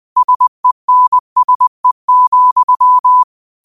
Morse code, heard as a keyed tone, about 20 wpm, 1000 Hz. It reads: SENSE,
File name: qotdfiles / 20wpm